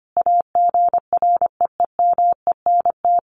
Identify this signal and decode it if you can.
Morse code, heard as a keyed tone, about 25 wpm, 700 Hz. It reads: AGREEMENT
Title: Sentences from Top 1500 Words 25wpm